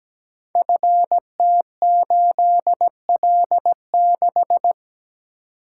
Morse code, heard as a keyed tone, about 17 wpm, 700 Hz. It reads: FT8L6
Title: Code Groups - 5 Letter:Numbers - 17wpm